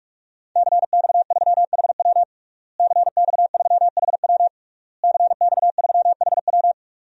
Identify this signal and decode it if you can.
Morse code, heard as a keyed tone, about 46 wpm, 700 Hz. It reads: CX3HW CX3HW CX3HW
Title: DXCC-100 Call Signs - 5 Characters - Repeated 3x - 45wpm